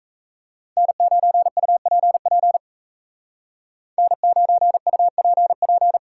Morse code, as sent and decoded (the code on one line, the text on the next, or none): -. ----. ..- .--. .--. / -. ----. ..- .--. .--.
N9UPP N9UPP